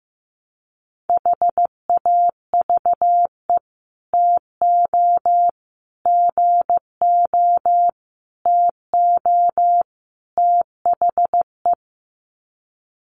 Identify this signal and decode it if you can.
Morse code, heard as a keyed tone, about 15 wpm, 700 Hz. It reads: HAVE TO GO TO THE
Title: Sets of 5 Words Encore 15wpm